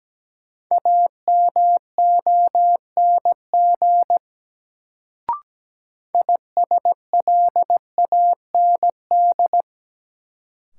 Morse code, heard as a keyed tone, about 17 wpm, 700 Hz. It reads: AMONG ISLAND
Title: Sentences from Top 500 Words - New Word Drill - 17wpm